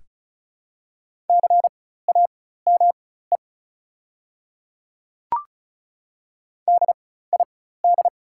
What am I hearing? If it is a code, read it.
Morse code, heard as a keyed tone, about 34 wpm, spaced as for 16 wpm, 700 Hz. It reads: CAME DID
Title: 4x Character Spacing - Top 200 Words - 35wpm